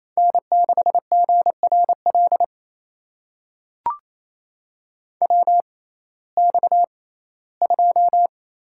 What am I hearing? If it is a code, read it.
Morse code, heard as a keyed tone, about 28 wpm, spaced as for 17 wpm, 700 Hz. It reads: N6GRL WX2
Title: USA Call Signs - 6x Character Spacing - 28wpm